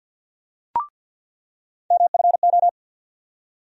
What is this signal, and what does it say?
Morse code, heard as a keyed tone, about 51 wpm, 700 Hz. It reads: MFO
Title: USA Call Sign Postfixes - 3 Letters 50wpm